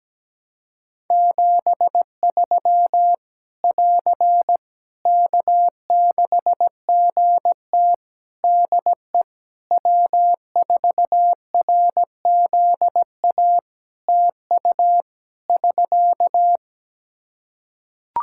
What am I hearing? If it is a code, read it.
Morse code, heard as a keyed tone, about 17 wpm, 700 Hz. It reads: 73 <AR> K6GT DE W4RZA TU <SK>
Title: Practice QSO - Exchange 4 - Ending - 17wpm